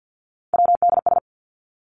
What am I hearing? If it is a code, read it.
Morse code, heard as a keyed tone, about 50 wpm, 700 Hz. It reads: WDS
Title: Abbreviations 50wpm